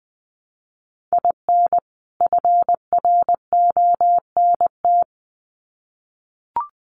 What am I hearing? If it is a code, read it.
Morse code, heard as a keyed tone, about 20 wpm, 700 Hz. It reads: IN FRONT
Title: Sets of 2 Words 20wpm